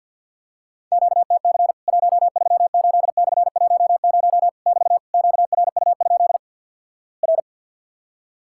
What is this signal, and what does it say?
Morse code, heard as a keyed tone, about 50 wpm, 700 Hz. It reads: QTC 138/10 <BT> QRU? R
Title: Worked All Europe DX Contest - QTC Exchange with Repeats - 50wpm